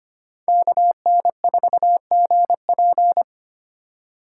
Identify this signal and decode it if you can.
Morse code, heard as a keyed tone, about 25 wpm, 700 Hz. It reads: KN4GP
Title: USA Call Signs - 4x Character Spacing - 25wpm